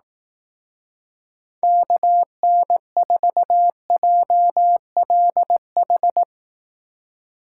Morse code, heard as a keyed tone, about 18 wpm, 700 Hz. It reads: KN4JLH
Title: Speed Racing - USA 2x3 Call Signs 15wpm